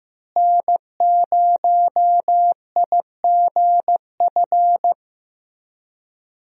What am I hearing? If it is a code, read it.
Morse code, heard as a keyed tone, about 15 wpm, 700 Hz. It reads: N0IGF
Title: USA Call Signs - 5 Characters - Repeated 3x - 15wpm